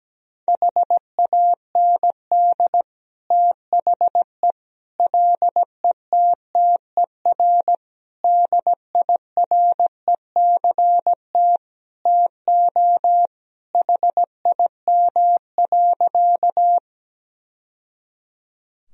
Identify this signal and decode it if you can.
Morse code, heard as a keyed tone, about 17 wpm, 700 Hz. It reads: HAND THE LETTER DIRECT TO HIM.
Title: Sentences from Top 1000 Words - Review Part 2 - 17wpm